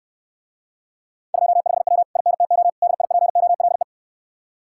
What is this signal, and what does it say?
Morse code, heard as a keyed tone, about 69 wpm, 700 Hz. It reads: 25W INTO DIPOLE